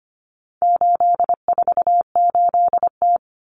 Morse code, heard as a keyed tone, about 25 wpm, 700 Hz. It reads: 848T
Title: Common Cut Numbers - 4-Digit 25wpm